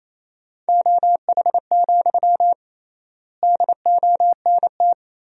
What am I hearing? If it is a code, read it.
Morse code, heard as a keyed tone, about 28 wpm, 700 Hz. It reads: OH, DONT
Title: 3x Word-Spacing - Sentences from Top 500 Words - 28wpm